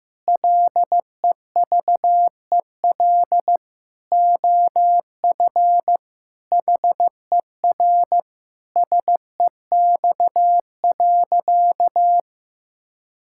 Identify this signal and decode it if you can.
Morse code, heard as a keyed tone, about 15 wpm, 700 Hz. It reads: LEVEL OF HER SEX.
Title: Sentences from Top 1600 Words 15wpm